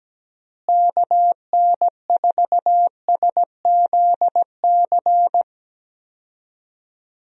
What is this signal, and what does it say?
Morse code, heard as a keyed tone, about 17 wpm, 700 Hz. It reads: KN4SZC
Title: USA Call Signs - 6 Characters - Repeated 2x - 17wpm